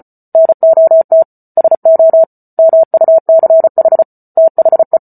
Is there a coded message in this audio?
Yes: Morse code from 0.3 to 5.0 s, heard as a keyed tone, about 34 wpm, 650 Hz. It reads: NOT SO MUCH THE